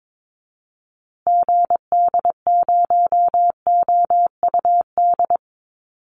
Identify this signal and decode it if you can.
Morse code, heard as a keyed tone, about 22 wpm, 700 Hz. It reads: GD0OUD
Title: Call Signs Repeated 3x - 22wpm